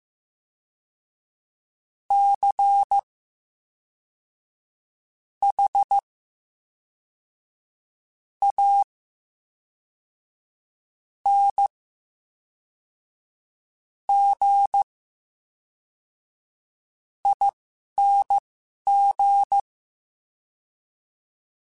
Morse code, heard as a keyed tone, about 15 wpm, spaced as for 11 wpm, 800 Hz. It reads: C H A N G ING
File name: Special INGs / Special INGs-mp3s / VerySlowToFaster Ing @10